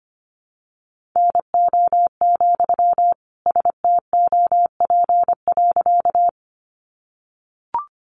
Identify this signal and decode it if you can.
Morse code, heard as a keyed tone, about 25 wpm, 700 Hz. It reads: NO, STOP.
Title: Sentences from Top 300 Words 25wpm